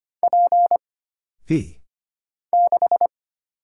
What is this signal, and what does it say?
Morse code, heard as a keyed tone, about 25 wpm, 700 Hz. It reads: P 6
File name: practice / 25wpm / Single Character (Letter:Number:Punctuation:Prosign) - Mind-Melt - 25wpm